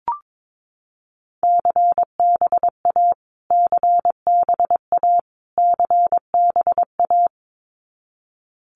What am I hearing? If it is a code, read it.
Morse code, heard as a keyed tone, about 22 wpm, 700 Hz. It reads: CBA CBA CBA
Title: Abbreviations Repeated 3x - 22wpm